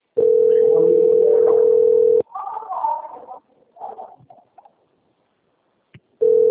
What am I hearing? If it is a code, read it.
Ringback tone (North America), 440 + 480 Hz.
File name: out- - - - - .20039